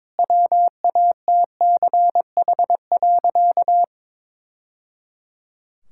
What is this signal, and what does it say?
Morse code, heard as a keyed tone, about 22 wpm, 700 Hz. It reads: WATCH.
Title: Sentences from Top 1000 Words - Review Part 5 - 22wpm